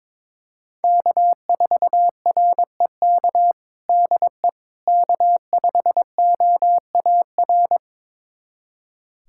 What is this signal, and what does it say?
Morse code, heard as a keyed tone, about 22 wpm, 700 Hz. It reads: K4REK DE K5OAR
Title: USA Callsign DE Callsign - 5 Characters - 22wpm